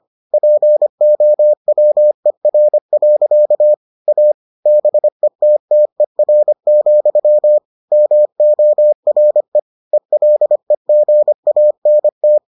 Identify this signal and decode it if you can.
Morse code, heard as a keyed tone, about 25 wpm, 600 Hz. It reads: POWER. A BETTER, MORE ELEGANT